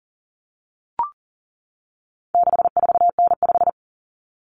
Morse code, heard as a keyed tone, about 38 wpm, 700 Hz. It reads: 64N5
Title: Common Cut Numbers - 4-Digit 40wpm